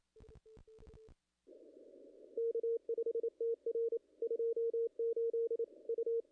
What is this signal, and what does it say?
Morse code, heard as a keyed tone, about 25 wpm, 440 Hz. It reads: K5TR 28 U
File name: contests / ss-cw-k5tr-op-n5rz -0000